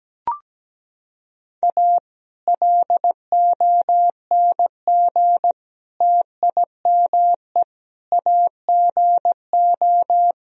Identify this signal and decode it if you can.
Morse code, heard as a keyed tone, about 17 wpm, 700 Hz. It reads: A LONG TIME AGO
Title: Sentences from Top 1000 Words - Review Part 3 - 17wpm